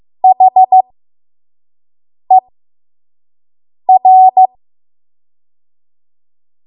Morse code, heard as a keyed tone, about 15 wpm, spaced as for 5 wpm, 750 Hz. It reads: HER
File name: Morse / 05WPM